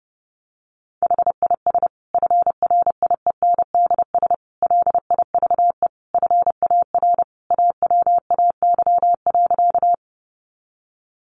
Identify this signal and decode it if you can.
Morse code, heard as a keyed tone, about 30 wpm, 700 Hz. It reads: HIS FRIENDS LIVE FAR AWAY.
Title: Additional Easy Phrases 30wpm